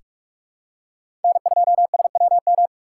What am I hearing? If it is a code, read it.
Morse code, heard as a keyed tone, about 46 wpm, 700 Hz. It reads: N2SWM
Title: USA Call Signs - 5 Characters - 4x Character Spacing - 45wpm